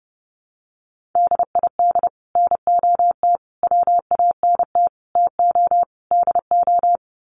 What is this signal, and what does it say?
Morse code, heard as a keyed tone, about 28 wpm, 700 Hz. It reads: DID NOT WANT TO DO
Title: Sets of 5 Words Encore 30wpm